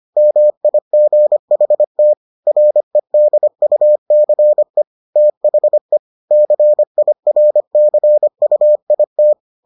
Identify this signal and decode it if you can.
Morse code, heard as a keyed tone, about 25 wpm, 600 Hz. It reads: MIGHT REDUCE THE CIRCUIT